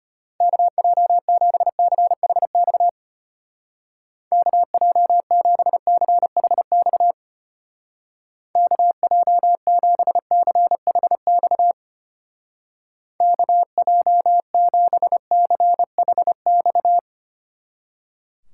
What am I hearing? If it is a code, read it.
Morse code, heard as a keyed tone, about 32 wpm, 700 Hz. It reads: KJ7CHX KJ7CHX KJ7CHX KJ7CHX